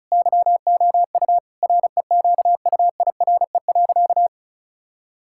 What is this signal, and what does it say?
Morse code, heard as a keyed tone, about 36 wpm, 700 Hz. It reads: YOU REQUIRE.
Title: Sentences from Top 1000 Words - Review Part 6 - 35wpm